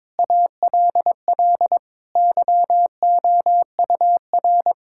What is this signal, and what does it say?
Morse code, heard as a keyed tone, about 22 wpm, 700 Hz. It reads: ALL YOUR
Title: Sentences from Top 1000 Words - Review Part 6 - 22wpm